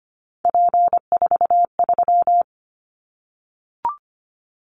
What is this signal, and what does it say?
Morse code, heard as a keyed tone, about 25 wpm, 700 Hz. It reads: P43
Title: Callsign Prefixes for Top 100 DXCC - 25wpm